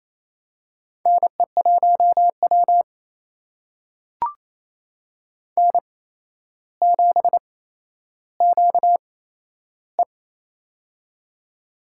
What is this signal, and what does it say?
Morse code, heard as a keyed tone, about 28 wpm, spaced as for 12 wpm, 700 Hz. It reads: NE1W N7QE
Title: USA Call Signs - 4 Characters - 8x Character Spacing - 28wpm